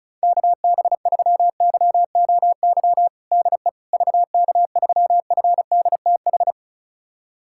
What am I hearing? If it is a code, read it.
Morse code, heard as a keyed tone, about 36 wpm, 700 Hz. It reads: KB3YOY DE VK3FDTH
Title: USA Callsign DE Callsign 35wpm